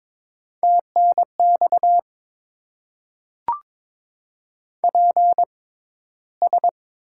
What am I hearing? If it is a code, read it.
Morse code, heard as a keyed tone, about 22 wpm, spaced as for 13 wpm, 700 Hz. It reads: TNX PS
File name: Top 90 QSO Elements - 6x Character Spacing - 22wpm